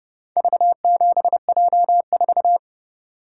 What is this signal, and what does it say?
Morse code, heard as a keyed tone, about 29 wpm, 700 Hz. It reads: V7J4